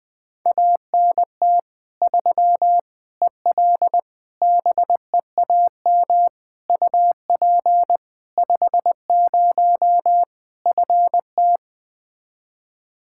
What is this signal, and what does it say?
Morse code, heard as a keyed tone, about 20 wpm, 700 Hz. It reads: ANT 3 EL BEAM UP 50 FT
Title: Practice QSO - Exchange 2 - Info - 20wpm